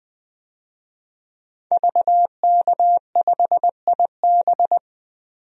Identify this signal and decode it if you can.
Morse code, heard as a keyed tone, about 21 wpm, 700 Hz. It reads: VK5IB
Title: DXCC-100 Call Signs - 5 Characters - 20wpm